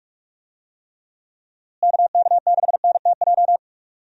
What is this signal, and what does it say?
Morse code, heard as a keyed tone, about 46 wpm, 700 Hz. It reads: KK6NTJ
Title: USA Call Signs - 6 Characters - Repeated 3x - 45wpm